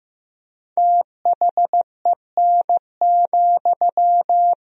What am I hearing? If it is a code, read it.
Morse code, heard as a keyed tone, about 16 wpm, 700 Hz. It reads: THEN,
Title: 4x Word-Spacing - Sentences from Top 100 Words - 15wpm